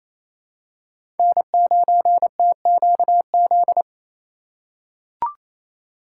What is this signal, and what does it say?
Morse code, heard as a keyed tone, about 28 wpm, 700 Hz. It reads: N9TQZ
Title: USA Call Signs - 5 Characters - 2x Character Spacing - 28wpm